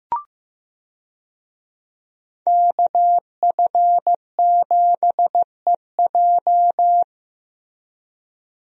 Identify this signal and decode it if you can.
Morse code, heard as a keyed tone, about 15 wpm, 700 Hz. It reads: KF7EJ
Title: USA Call Signs - 5 Characters - Repeated 3x - 15wpm